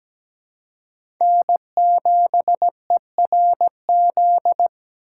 Morse code, heard as a keyed tone, about 17 wpm, 700 Hz. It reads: N7ERZ
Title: USA Call Signs - 5 Characters - 2x Character Spacing - 17wpm